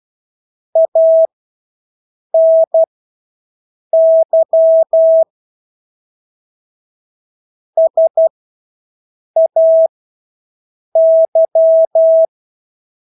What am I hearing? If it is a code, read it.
Morse code, heard as a keyed tone, about 12 wpm, spaced as for 6 wpm, 650 Hz. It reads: ANY SAY
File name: Engelske ord 12-6 wpm